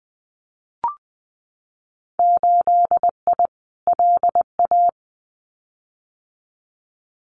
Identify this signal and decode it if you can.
Morse code, heard as a keyed tone, about 20 wpm, 700 Hz. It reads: 8I LA
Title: Winter Field Day Exchanges Encore 20wpm